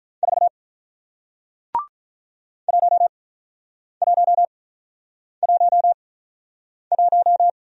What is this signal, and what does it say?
Morse code, heard as a keyed tone, about 44 wpm, spaced as for 9 wpm, 700 Hz. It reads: 4 1111
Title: Speed-Racing Single Letter-Number 35wpm